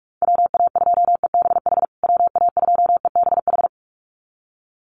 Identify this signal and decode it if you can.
Morse code, heard as a keyed tone, about 44 wpm, 700 Hz. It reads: WA2EBH WA2EBH
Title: USA Call Signs Repeated 2x 45wpm